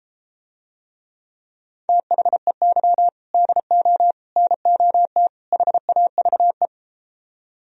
Morse code, heard as a keyed tone, about 31 wpm, 700 Hz. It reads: THEY DO NOT HAVE